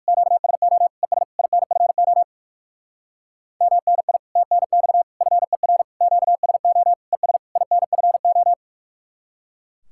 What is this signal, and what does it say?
Morse code, heard as a keyed tone, about 60 wpm, 700 Hz. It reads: QSO ES INFO MNI TNX FER QSO ES INFO